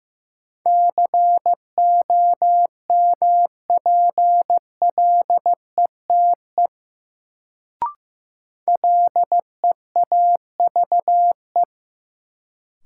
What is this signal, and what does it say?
Morse code, heard as a keyed tone, about 16 wpm, 700 Hz. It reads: COMPLETE LEAVE
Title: Sentences from Top 400 Words - New Word Drill - 15wpm